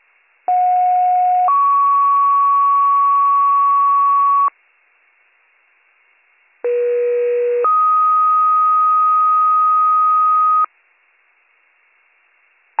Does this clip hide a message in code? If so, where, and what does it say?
Yes: two-tone page from 0.5 to 10.7 s, heard as a one-second tone, then a three-second tone, about 730 then 1120 Hz, about 480 then 1250 Hz.